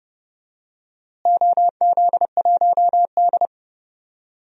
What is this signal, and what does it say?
Morse code, heard as a keyed tone, about 31 wpm, 700 Hz. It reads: OZ1D
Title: DXCC-100 Call Signs - 4 Characters - Repeated 2x - 30wpm